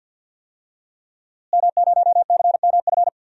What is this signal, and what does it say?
Morse code, heard as a keyed tone, about 48 wpm, 700 Hz. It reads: M0CMF
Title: DXCC-100 Call Signs - 5 Characters - Repeated 2x - 50wpm